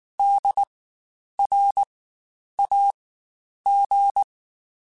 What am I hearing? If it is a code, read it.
Morse code, heard as a keyed tone, about 19 wpm, spaced as for 9 wpm, 800 Hz. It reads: DRAG
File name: Spaced Alphas / Spaced Alphas mp3s / Ds sp @18